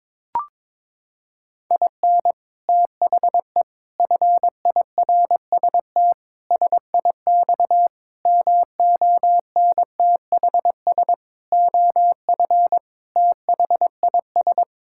Morse code, heard as a keyed tone, about 22 wpm, 700 Hz. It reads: IN THE FIRST SIX MONTHS OF THIS
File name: practice / 22wpm / Sets of 8 Words - 22wpm